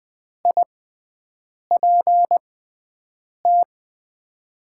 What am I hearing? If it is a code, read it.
Morse code, heard as a keyed tone, about 20 wpm, spaced as for 7 wpm, 700 Hz. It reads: IPT
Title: USA Call Signs - 6 Characters - 6x Character Spacing - 20wpm